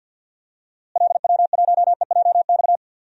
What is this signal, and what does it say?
Morse code, heard as a keyed tone, about 53 wpm, 700 Hz. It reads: LW1EJX
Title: DXCC-100 Call Signs - 6 Characters - Repeated 2x - 50wpm